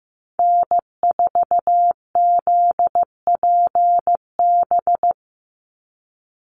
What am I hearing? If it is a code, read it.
Morse code, heard as a keyed tone, about 15 wpm, 700 Hz. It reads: N4ZPB
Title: USA Call Signs 15wpm